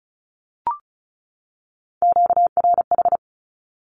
Morse code, heard as a keyed tone, about 34 wpm, 700 Hz. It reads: QRH
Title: Q-Codes 35wpm